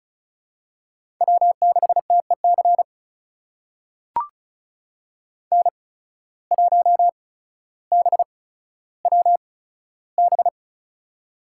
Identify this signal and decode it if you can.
Morse code, heard as a keyed tone, about 36 wpm, spaced as for 15 wpm, 700 Hz. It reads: W6TEC N1BWB
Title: USA Call Signs - 5 Characters - 8x Character Spacing - 35wpm